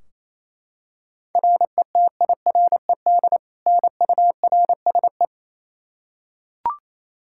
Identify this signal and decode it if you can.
Morse code, heard as a keyed tone, about 28 wpm, 700 Hz. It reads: RETIRED NURSE
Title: Practice QSO - Exchange 3 - More Info - 28wpm